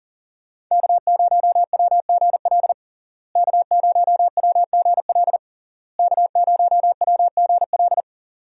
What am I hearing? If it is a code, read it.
Morse code, heard as a keyed tone, about 40 wpm, 700 Hz. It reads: K0WGL K0WGL K0WGL
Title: USA Call Signs - 5 Characters - Repeated 3x - 40wpm